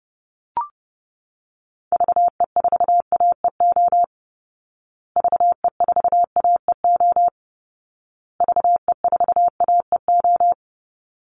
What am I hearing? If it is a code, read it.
Morse code, heard as a keyed tone, about 30 wpm, 700 Hz. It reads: VE4AEO VE4AEO VE4AEO
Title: Call Signs Repeated 2x - 30wpm